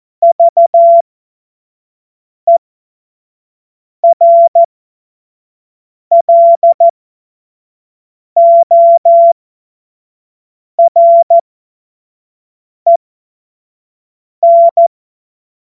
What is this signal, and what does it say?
Morse code, heard as a keyed tone, about 14 wpm, spaced as for 5 wpm, 680 Hz. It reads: VERLOREN
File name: Morsekurs / Witze / Witz 09. Februar 2021, 25 F70